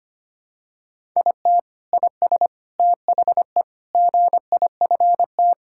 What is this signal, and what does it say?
Morse code, heard as a keyed tone, about 25 wpm, 700 Hz. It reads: IT IS THE GIFT
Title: Sentences from Top 1000 Words - Review Part 6 - 25wpm